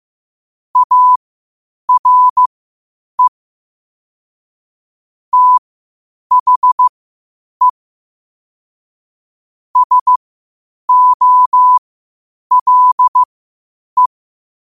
Morse code, heard as a keyed tone, about 15 wpm, spaced as for 8 wpm, 1000 Hz. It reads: ARE THE SOLE
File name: qotdfiles / 8wpm